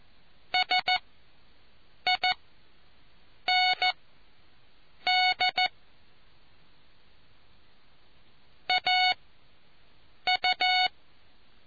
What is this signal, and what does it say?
Morse code, heard as a keyed tone, about 14 wpm, spaced as for 6 wpm, 740 Hz. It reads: SIND AU